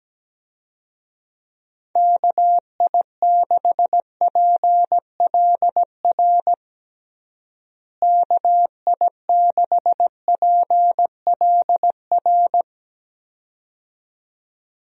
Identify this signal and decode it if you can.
Morse code, heard as a keyed tone, about 17 wpm, 700 Hz. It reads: KI6PLR KI6PLR
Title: USA Call Signs - 6 Characters - Repeated 2x - 17wpm